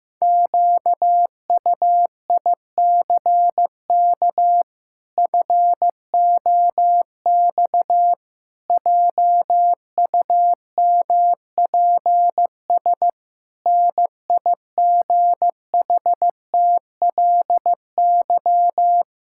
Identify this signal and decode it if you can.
Morse code, heard as a keyed tone, about 15 wpm, 700 Hz. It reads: QUICK FOX JUMPS NIGHTLY
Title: Pangram Challenge v2 - 15wpm